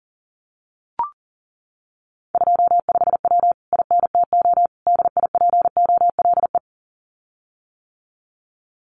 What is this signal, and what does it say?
Morse code, heard as a keyed tone, about 40 wpm, 700 Hz. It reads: 25W INTO DIPOLE
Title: Common QSO Phrases Encore 40wpm